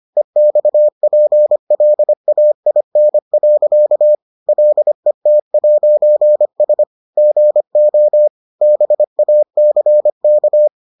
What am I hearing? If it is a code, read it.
Morse code, heard as a keyed tone, about 25 wpm, 600 Hz. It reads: EXPLAIN. LET'S GO BACK